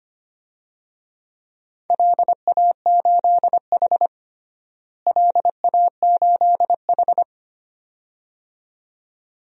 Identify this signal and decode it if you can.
Morse code, heard as a keyed tone, about 25 wpm, 700 Hz. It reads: LA8H LA8H
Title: DXCC-100 Call Signs - 4 Characters - Repeated 2x - 25wpm